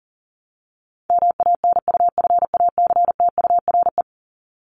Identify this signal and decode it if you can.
Morse code, heard as a keyed tone, about 40 wpm, 700 Hz. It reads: MANUFACTURE
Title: ING Suffix - Top 100 Words with Base Words 40wpm